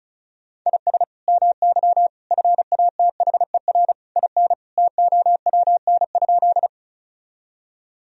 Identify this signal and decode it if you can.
Morse code, heard as a keyed tone, about 34 wpm, 700 Hz. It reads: IS MY FATHER IN TOWN?
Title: Sentences from Top 1000 Words - Review Part 2 - 35wpm